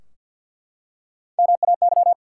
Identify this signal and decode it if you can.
Morse code, heard as a keyed tone, about 47 wpm, 700 Hz. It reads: MAY
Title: Three Letter Words - 50wpm